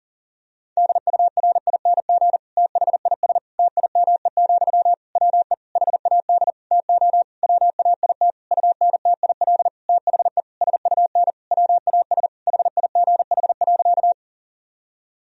Morse code, heard as a keyed tone, about 40 wpm, 700 Hz. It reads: DURING THIS TIME, WE HAD TO WAIT UNTIL THE SUN WAS HIGH.
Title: Sentences from Top 1000 Words - Review Part 2 - 40wpm